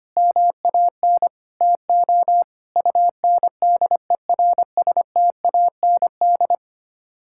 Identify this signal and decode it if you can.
Morse code, heard as a keyed tone, about 25 wpm, 700 Hz. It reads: MAN TO UNDERSTAND
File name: Sets of 9 Words - 25wpm